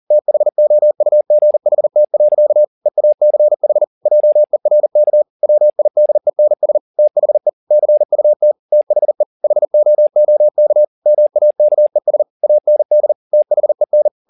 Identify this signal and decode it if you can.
Morse code, heard as a keyed tone, about 40 wpm, 600 Hz. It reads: THOUGHT. EACH JERK WIDENS THE CUT THE HOOK MAKES AND THEN